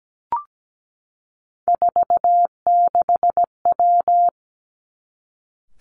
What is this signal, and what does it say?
Morse code, heard as a keyed tone, about 17 wpm, 700 Hz. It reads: 46W
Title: Three Challenging Letter-Number Combinations 17wpm